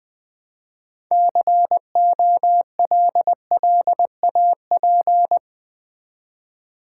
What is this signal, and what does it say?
Morse code, heard as a keyed tone, about 20 wpm, 700 Hz. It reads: COLLAP
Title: Build A Word - 2 Letters At A Time - 8-Letter Words - 20wpm